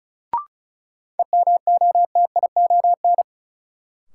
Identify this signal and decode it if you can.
Morse code, heard as a keyed tone, about 36 wpm, 700 Hz. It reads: EMOTION
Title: Sentences from Top 1600 Words - New Word Drill - 35wpm